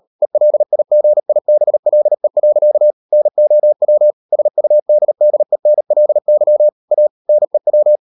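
Morse code, heard as a keyed tone, about 38 wpm, 600 Hz. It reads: ELIGIBLE. NOW SUDDENLY A NEW
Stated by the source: Morse code practice at 38 WPM, sending Ken Follett's novel Never in English, from CW med Gnister.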